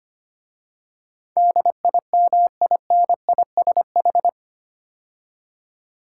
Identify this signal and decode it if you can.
Morse code, heard as a keyed tone, about 25 wpm, 700 Hz. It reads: DIMINISH
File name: 8 Letter Words - First 3 Letters Given - 25wpm